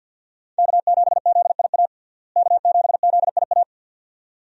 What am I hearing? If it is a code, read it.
Morse code, heard as a keyed tone, about 48 wpm, 700 Hz. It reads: K7ZIA K7ZIA
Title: USA Call Signs - 5 Characters - Repeated 2x - 50wpm